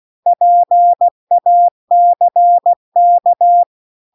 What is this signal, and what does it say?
Morse code, heard as a keyed tone, about 16 wpm, 700 Hz. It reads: PACK
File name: CW-The-Big-Twoheaded-River-2-16WPM